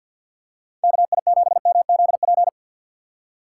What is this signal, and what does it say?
Morse code, heard as a keyed tone, about 51 wpm, 700 Hz. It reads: KI7MZP
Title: USA Call Signs - 6 Characters - 4x Character Spacing - 50wpm